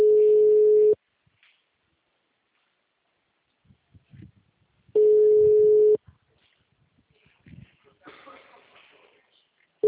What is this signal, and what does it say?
Ringback tone (Europe, 425 Hz), with speech.
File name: out- -2008- - - .24791